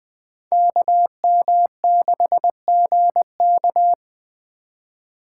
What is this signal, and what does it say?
Morse code, heard as a keyed tone, about 21 wpm, 700 Hz. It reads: KM6GK
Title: Speed Racing - USA 2x2 Call Signs 17wpm